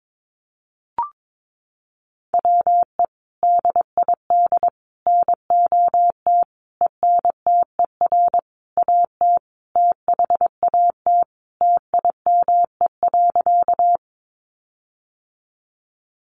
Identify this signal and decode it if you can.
Morse code, heard as a keyed tone, about 22 wpm, 700 Hz. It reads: WE DID NOT ENTER AT THAT TIME.
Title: Sentences from Top 1300 Words 22wpm